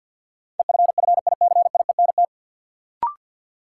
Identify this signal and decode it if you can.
Morse code, heard as a keyed tone, about 53 wpm, 700 Hz. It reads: EFFICIENT
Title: Sentences from Top 1300 Words - New Word Drill - 50wpm